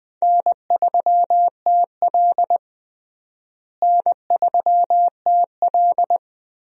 Morse code, heard as a keyed tone, about 20 wpm, 700 Hz. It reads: N3TL N3TL
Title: USA Call Signs - 4 Characters - Repeated 2x - 20wpm